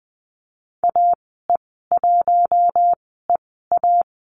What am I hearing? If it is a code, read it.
Morse code, heard as a keyed tone, about 20 wpm, spaced as for 15 wpm, 700 Hz. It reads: AE1EA
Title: 042 Lesson 8g - Number 1 - Call Signs 20wpm